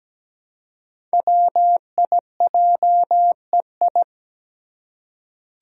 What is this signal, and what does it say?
Morse code, heard as a keyed tone, about 17 wpm, 700 Hz. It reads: WIJEI
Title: Five Challenging Letter-Number Combinations 17wpm